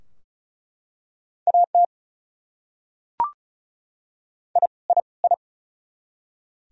Morse code, heard as a keyed tone, about 36 wpm, 700 Hz. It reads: AT I I I
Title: Top 100 Words Repeated 35wpm